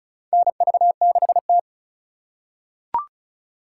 Morse code, heard as a keyed tone, about 34 wpm, 700 Hz. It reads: NV6T
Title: USA Call Signs - 4 Characters - 2x Character Spacing - 35wpm